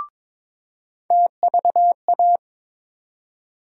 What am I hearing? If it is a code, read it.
Morse code, heard as a keyed tone, about 22 wpm, 700 Hz. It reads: TVA
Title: USA Call Sign Postfixes - 3 Letters 22wpm